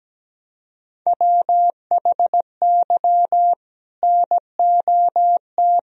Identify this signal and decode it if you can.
Morse code, heard as a keyed tone, about 17 wpm, 700 Hz. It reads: WHY NOT
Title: Sentences from Top 1000 Words - Review Part 1 - 17wpm